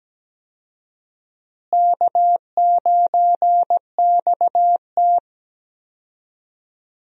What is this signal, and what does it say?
Morse code, heard as a keyed tone, about 17 wpm, 700 Hz. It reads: K9XT
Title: USA Call Signs - 4 Characters - Repeated 2x - 17wpm